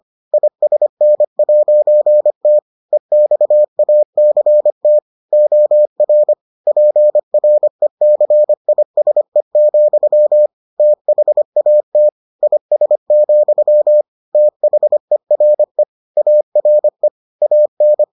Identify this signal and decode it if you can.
Morse code, heard as a keyed tone, about 25 wpm, 600 Hz. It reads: ISN'T EXACT OR PRECISE, THAT IS, THERE ARE AN